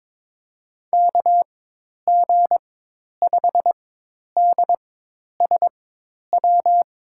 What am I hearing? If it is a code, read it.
Morse code, heard as a keyed tone, about 22 wpm, spaced as for 10 wpm, 700 Hz. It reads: KG5DSW
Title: USA Call Signs - 6 Characters - 4x Character Spacing - 22wpm